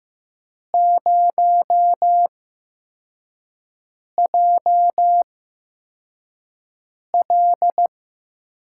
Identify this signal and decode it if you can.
Morse code, heard as a keyed tone, about 15 wpm, spaced as for 4 wpm, 700 Hz. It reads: 0JL
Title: USA Call Signs - 8x Character Spacing - 15wpm